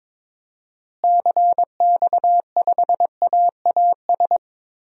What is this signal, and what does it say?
Morse code, heard as a keyed tone, about 22 wpm, 700 Hz. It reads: CX5AAS
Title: DXCC-100 Call Signs - 6 Characters - Repeated 3x - 22wpm